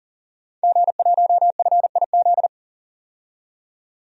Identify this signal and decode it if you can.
Morse code, heard as a keyed tone, about 40 wpm, 700 Hz. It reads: G1FIZ
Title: DXCC-100 Call Signs - 5 Characters - 40wpm